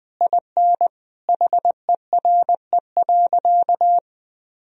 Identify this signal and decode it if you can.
Morse code, heard as a keyed tone, about 20 wpm, 700 Hz. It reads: IN HERE.
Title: Sentences from Top 1000 Words - Review Part 4 - 20wpm